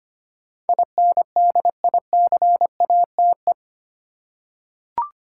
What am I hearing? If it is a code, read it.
Morse code, heard as a keyed tone, about 25 wpm, 700 Hz. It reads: INDICATE
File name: 8 Letter Words - First 3 Letters Given - 25wpm